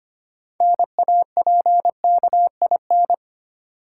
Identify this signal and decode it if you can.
Morse code, heard as a keyed tone, about 25 wpm, 700 Hz. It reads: NAPKIN
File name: practice / 25wpm / Easy Words with Hints - 25wpm